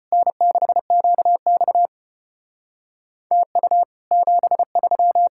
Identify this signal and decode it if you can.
Morse code, heard as a keyed tone, about 33 wpm, 700 Hz. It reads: N6QX TU 73